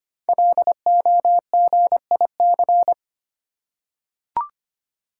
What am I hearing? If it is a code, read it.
Morse code, heard as a keyed tone, about 25 wpm, 700 Hz. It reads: LOGIC
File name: 5 Letter Words - First 3 Letters Given - 25wpm